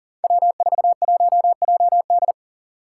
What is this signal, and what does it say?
Morse code, heard as a keyed tone, about 37 wpm, 700 Hz. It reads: W41JD
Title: Five Challenging Letter-Number Combinations 40wpm